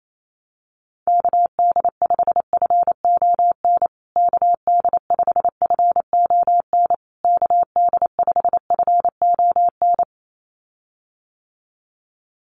Morse code, heard as a keyed tone, about 28 wpm, 700 Hz. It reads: KD5FON KD5FON KD5FON
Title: USA Call Signs Repeated 3x 28wpm